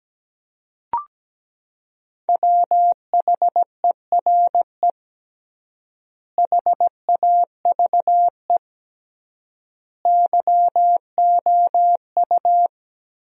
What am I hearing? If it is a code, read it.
Morse code, heard as a keyed tone, about 17 wpm, 700 Hz. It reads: WHERE HAVE YOU
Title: 3x Word-Spacing - Sentences from Top 200 Words - 17wpm